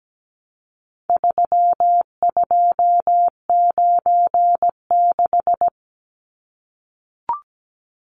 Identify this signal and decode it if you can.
Morse code, heard as a keyed tone, about 17 wpm, 700 Hz. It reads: 3296
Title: Numbers - 4-Digit 17wpm